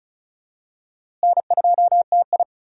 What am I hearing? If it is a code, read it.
Morse code, heard as a keyed tone, about 36 wpm, 700 Hz. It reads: N2TI
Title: USA Call Signs - 4 Characters - Repeated 2x - 35wpm